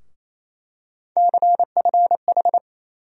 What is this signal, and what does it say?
Morse code, heard as a keyed tone, about 28 wpm, 700 Hz. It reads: CFH
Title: USA Call Sign Postfixes - 3 Letters 28wpm